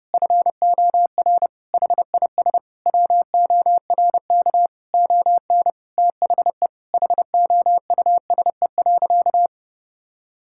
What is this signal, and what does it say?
Morse code, heard as a keyed tone, about 30 wpm, 700 Hz. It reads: FOR HIS WORK ON THE HOUSE.
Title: Sentences from Top 1000 Words - Review Part 5 - 30wpm